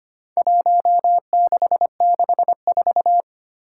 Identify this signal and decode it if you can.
Morse code, heard as a keyed tone, about 25 wpm, 700 Hz. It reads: 1664
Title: Four Challenging Letter-Number Combinations 25wpm